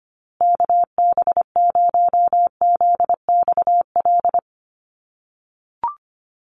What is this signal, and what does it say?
Morse code, heard as a keyed tone, about 25 wpm, 700 Hz. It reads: KB0ZXL
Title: USA Call Signs Repeated 2x 25wpm